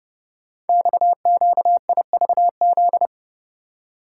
Morse code, heard as a keyed tone, about 28 wpm, 700 Hz. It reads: XQIVZ
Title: Code Groups - 5 Letters - 30wpm